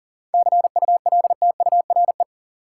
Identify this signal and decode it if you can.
Morse code, heard as a keyed tone, about 40 wpm, 700 Hz. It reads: CULTURE
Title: Seven Letter Words - 40wpm